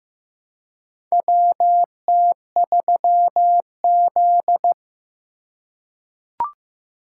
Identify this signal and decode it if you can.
Morse code, heard as a keyed tone, about 15 wpm, 700 Hz. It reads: WT3Z
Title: USA Call Signs - 4 Characters - 3x Character Spacing - 15wpm